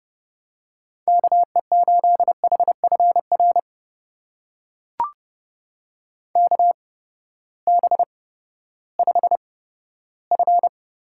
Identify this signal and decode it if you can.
Morse code, heard as a keyed tone, about 31 wpm, spaced as for 15 wpm, 700 Hz. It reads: KE8HFR KB5F
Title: USA Call Signs - 6 Characters - 8x Character Spacing - 30wpm